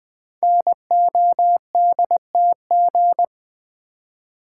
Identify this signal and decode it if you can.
Morse code, heard as a keyed tone, about 20 wpm, 700 Hz. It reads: NODTG
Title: Code Groups - 5 Letters - 20wpm